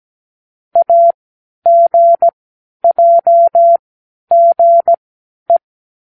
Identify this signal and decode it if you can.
Morse code, heard as a keyed tone, about 17 wpm, spaced as for 10 wpm, 680 Hz. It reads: AGJGE
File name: k7qo / Lesson15(A-J)